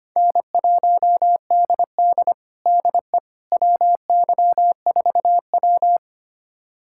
Morse code, heard as a keyed tone, about 25 wpm, 700 Hz. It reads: N1DD DE WY4W
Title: USA Callsign DE Callsign - 4 Characters - 25wpm